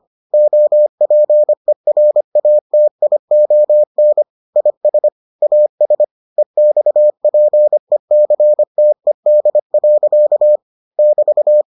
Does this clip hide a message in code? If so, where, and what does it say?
Yes: Morse code from 0.3 to 11.6 s, heard as a keyed tone, about 25 wpm, 600 Hz. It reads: OPERATION IS AS EXPECTED. <BT>